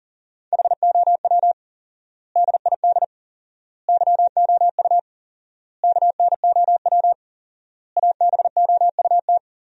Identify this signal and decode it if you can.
Morse code, heard as a keyed tone, about 40 wpm, 700 Hz. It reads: HOW DID YOU KNOW ABOUT
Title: 4x Word-Spacing - Sentences from Top 100 Words - 40wpm